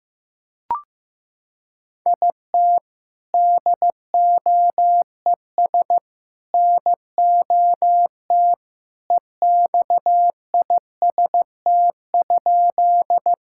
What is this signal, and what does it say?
Morse code, heard as a keyed tone, about 15 wpm, 700 Hz. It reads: IT DOES NOT EXIST?
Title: Sentences from Top 1000 Words - Review Part 5 - 15wpm